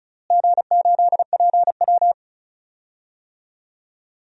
Morse code, heard as a keyed tone, about 34 wpm, 700 Hz. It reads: G8PW
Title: DXCC-100 Call Signs - 4 Characters - Repeated 3x - 35wpm